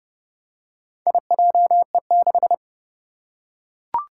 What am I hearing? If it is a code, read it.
Morse code, heard as a keyed tone, about 32 wpm, 700 Hz. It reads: IJE6
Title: Four Challenging Letter-Number Combinations 30wpm